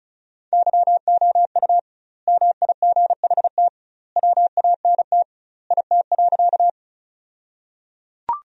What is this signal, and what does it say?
Morse code, heard as a keyed tone, about 36 wpm, 700 Hz. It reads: YOU MIGHT WANT IT.
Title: 2x Word-Spacing - Sentences from Top 300 Words - 35wpm